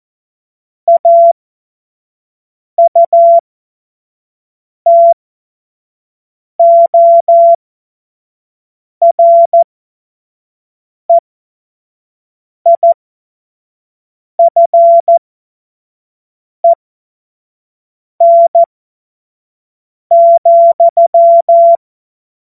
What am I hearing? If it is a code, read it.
Morse code, heard as a keyed tone, about 14 wpm, spaced as for 5 wpm, 680 Hz. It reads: AUTOREIFEN,
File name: Morsekurs / Witze / Witz 19. April 2022 25 F70